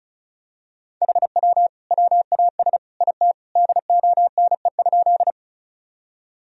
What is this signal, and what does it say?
Morse code, heard as a keyed tone, about 34 wpm, 700 Hz. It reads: HW WAS IT DONE?
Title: Sentences from Top 500 Words and Abbreviations 35wpm